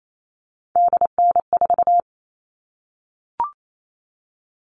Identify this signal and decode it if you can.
Morse code, heard as a keyed tone, about 28 wpm, 700 Hz. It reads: DN4
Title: All Cut Numbers - 3-Digit 28wpm